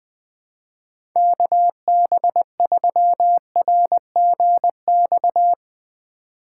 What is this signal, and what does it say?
Morse code, heard as a keyed tone, about 20 wpm, 700 Hz. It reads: KB3RGX
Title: USA Call Signs - 6 Characters - 4x Character Spacing - 20wpm